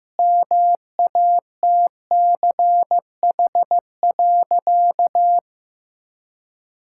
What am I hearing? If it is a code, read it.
Morse code, heard as a keyed tone, about 15 wpm, 700 Hz. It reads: MATCH.
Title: Sentences from Top 1600 Words 15wpm